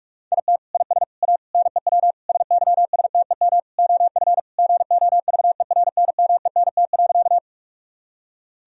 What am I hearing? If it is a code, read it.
Morse code, heard as a keyed tone, about 47 wpm, 700 Hz. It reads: IT IS A NEW SYSTEM OF GOVERNMENT.
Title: Sentences from Top 1000 Words - Review Part 3 - 45wpm